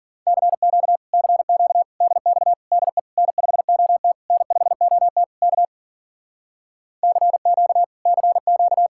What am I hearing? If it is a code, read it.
Morse code, heard as a keyed tone, about 50 wpm, 700 Hz. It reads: CQ CQ DX DE N5OT N5OT K CQ CQ
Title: Speed-Racing Common QSO Phrases Encore - 40wpm